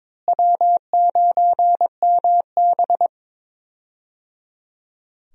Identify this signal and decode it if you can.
Morse code, heard as a keyed tone, about 22 wpm, 700 Hz. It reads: W9MB
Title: USA Call Signs - 4 Characters - Repeated 3x - 22wpm